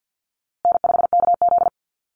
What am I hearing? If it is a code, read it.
Morse code, heard as a keyed tone, about 48 wpm, 700 Hz. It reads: N5KZ
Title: USA Call Signs Repeated 3x 50wpm